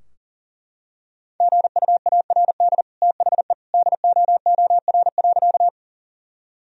Morse code, heard as a keyed tone, about 40 wpm, 700 Hz. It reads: GUARD THE DOOR.
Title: Sentences from Top 1000 Words - Review Part 4 - 40wpm